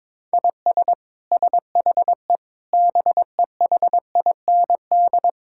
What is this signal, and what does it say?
Morse code, heard as a keyed tone, about 22 wpm, 700 Hz. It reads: IS SHE BEHIND
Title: Sentences from Top 1000 Words - Review Part 6 - 22wpm